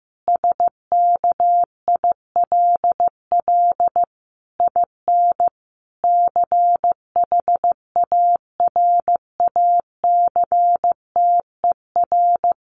Morse code, heard as a keyed tone, about 15 wpm, 700 Hz. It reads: SKILL IN CHARACTER
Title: Sentences from Top 1200 Words 15wpm